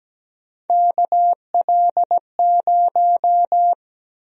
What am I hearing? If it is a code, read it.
Morse code, heard as a keyed tone, about 17 wpm, 700 Hz. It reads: KL0
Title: USA Call Sign Prefixes - 17wpm